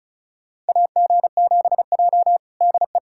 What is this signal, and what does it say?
Morse code, heard as a keyed tone, about 34 wpm, 700 Hz. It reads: AG7J DE
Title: USA Callsign DE Callsign - 4 Characters - 35wpm